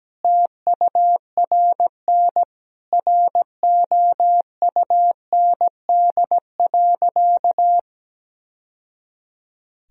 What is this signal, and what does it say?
Morse code, heard as a keyed tone, about 17 wpm, 700 Hz. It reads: TURN ROUND.
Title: Sentences from Top 1000 Words - Review Part 1 - 17wpm